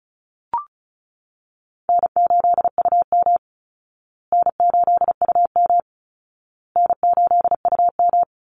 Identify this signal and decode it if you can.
Morse code, heard as a keyed tone, about 34 wpm, 700 Hz. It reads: N8UM N8UM N8UM
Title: Call Signs Repeated 2x - 35wpm